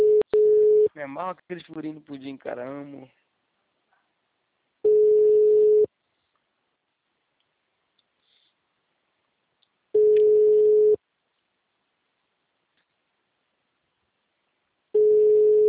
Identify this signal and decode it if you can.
Ringback tone (Europe, 425 Hz), with speech.